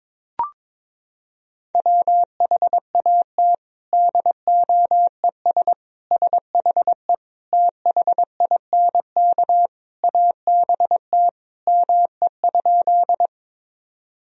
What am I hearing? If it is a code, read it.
Morse code, heard as a keyed tone, about 22 wpm, 700 Hz. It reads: WHAT DOES SHE THINK ABT ME?
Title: Sentences from Top 200 Words and Abbreviations 22wpm